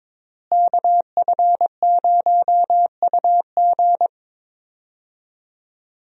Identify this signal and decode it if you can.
Morse code, heard as a keyed tone, about 22 wpm, 700 Hz. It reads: KF0UG
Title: USA Call Signs - 5 Characters - Repeated 3x - 22wpm